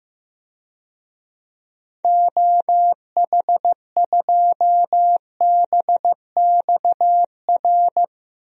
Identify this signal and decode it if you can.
Morse code, heard as a keyed tone, about 16 wpm, 700 Hz. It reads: OH2BXR
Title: DXCC-100 Call Signs - 6 Characters - Repeated 2x - 15wpm